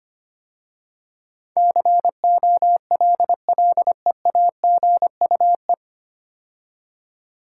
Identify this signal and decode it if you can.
Morse code, heard as a keyed tone, about 25 wpm, 690 Hz. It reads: COLLEAGUE
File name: practice / 25wpm / 9 Letter Words - First 3 Letters Given - 25wpm